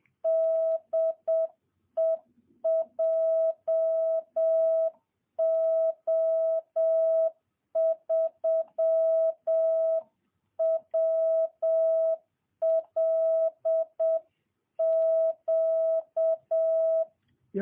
Morse code, heard as a keyed tone, about 7 wpm, 650 Hz. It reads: DE JO3WLQ
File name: jh8cbh / 2022cw